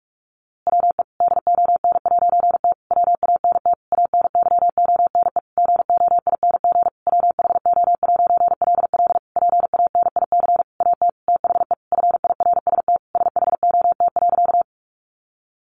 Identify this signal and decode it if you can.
Morse code, heard as a keyed tone, about 44 wpm, 700 Hz. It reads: WE DON'T WANT ANYONE GOING WHO'LL PANIC AT THE FIRST SHOT.
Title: A Sound Of Thunder 45wpm